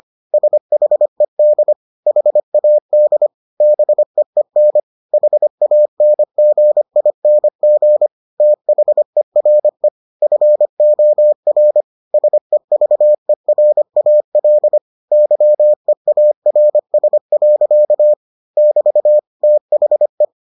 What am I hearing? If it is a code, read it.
Morse code, heard as a keyed tone, about 25 wpm, 600 Hz. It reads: SHED HAD BEEN HANGING THERE FOR SEVERAL YEARS. <BT> THE